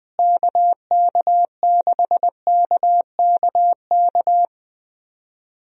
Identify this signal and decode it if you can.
Morse code, heard as a keyed tone, about 20 wpm, 700 Hz. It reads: KK6KKK
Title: USA Call Signs - 6 Characters - Repeated 3x - 20wpm